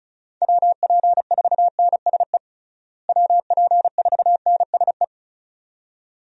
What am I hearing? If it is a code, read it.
Morse code, heard as a keyed tone, about 34 wpm, 700 Hz. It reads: WP4NSE WP4NSE
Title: USA Call Signs - 6 Characters - Repeated 2x - 35wpm